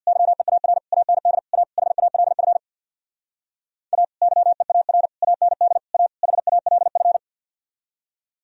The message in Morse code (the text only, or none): YEAR AND A HALF A YEAR AND A HALF